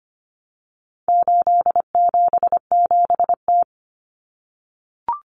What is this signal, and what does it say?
Morse code, heard as a keyed tone, about 25 wpm, 700 Hz. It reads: 877T
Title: Common Cut Numbers - 4-Digit 25wpm